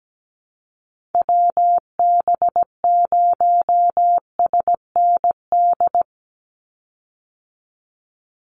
Morse code, heard as a keyed tone, about 17 wpm, 700 Hz. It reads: WB0SND
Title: Call Signs Repeated 3x - 17wpm